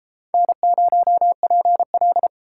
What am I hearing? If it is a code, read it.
Morse code, heard as a keyed tone, about 28 wpm, 700 Hz. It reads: N0PL